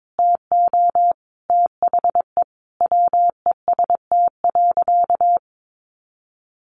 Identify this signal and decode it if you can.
Morse code, heard as a keyed tone, about 22 wpm, 700 Hz. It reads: TO THE WEST.
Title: Sentences from Top 1200 Words 22wpm